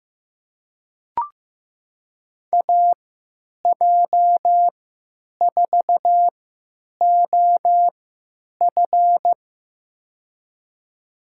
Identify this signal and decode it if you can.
Morse code, heard as a keyed tone, about 15 wpm, spaced as for 9 wpm, 700 Hz. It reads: AJ4OF
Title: USA Call Signs - 5 Characters - 3x Character Spacing - 15wpm